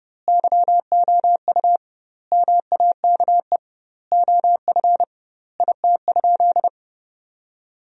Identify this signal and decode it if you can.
Morse code, heard as a keyed tone, about 30 wpm, 700 Hz. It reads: YOU MAKE OF IT?
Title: 2x Word-Spacing - Sentences from Top 100 Words - 30wpm